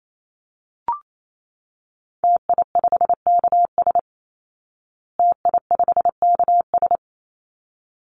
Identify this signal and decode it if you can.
Morse code, heard as a keyed tone, about 28 wpm, 700 Hz. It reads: TI5KS TI5KS
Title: Call Signs Repeated 3x - 28wpm